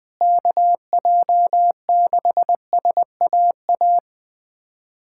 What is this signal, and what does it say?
Morse code, heard as a keyed tone, about 21 wpm, 700 Hz. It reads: KJ6SAA
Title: USA Callsign DE Callsign 20wpm